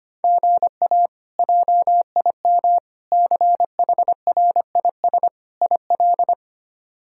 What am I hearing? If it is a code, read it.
Morse code, heard as a keyed tone, about 25 wpm, 700 Hz. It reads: GA JIM CHRIS IL
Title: K1USN SST QSO Practice - 25wpm